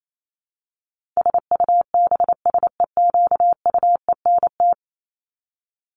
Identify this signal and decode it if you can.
Morse code, heard as a keyed tone, about 28 wpm, 700 Hz. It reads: SUBSEQUENT
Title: LY Suffix - Top 100 Words with Base Words 28wpm